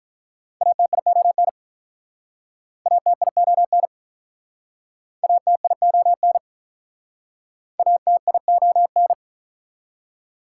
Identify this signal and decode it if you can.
Morse code, heard as a keyed tone, about 44 wpm, 690 Hz. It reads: ATION ATION ATION ATION